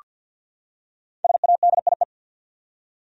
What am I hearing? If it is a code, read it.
Morse code, heard as a keyed tone, about 47 wpm, 700 Hz. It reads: SADIE
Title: USA Common Names - Max 5 Letters - 50wpm